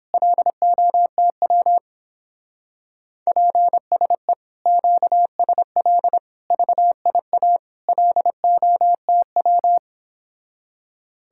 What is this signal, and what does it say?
Morse code, heard as a keyed tone, about 29 wpm, 700 Hz. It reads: LOTW PSE QSL VIA LOTW